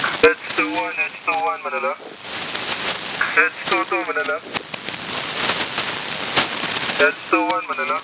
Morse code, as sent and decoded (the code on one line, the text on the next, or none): none